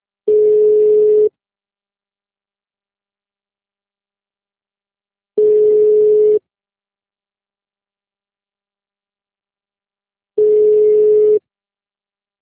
Ringback tone (Europe, 425 Hz).
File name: out- -2006- - - .204